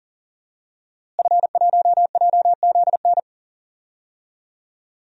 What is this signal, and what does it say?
Morse code, heard as a keyed tone, about 40 wpm, 700 Hz. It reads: F1JZN
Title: DXCC-100 Call Signs - 5 Characters - Repeated 2x - 40wpm